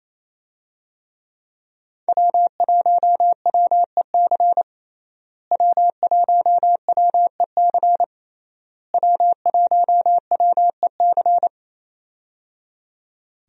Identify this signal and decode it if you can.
Morse code, heard as a keyed tone, about 27 wpm, 700 Hz. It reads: W1WEC W1WEC W1WEC
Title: USA Call Signs - 5 Characters - Repeated 3x - 28wpm